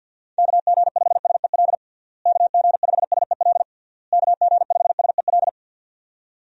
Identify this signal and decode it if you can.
Morse code, heard as a keyed tone, about 51 wpm, 700 Hz. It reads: KG5SEL KG5SEL KG5SEL
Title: USA Call Signs - 6 Characters - Repeated 3x - 50wpm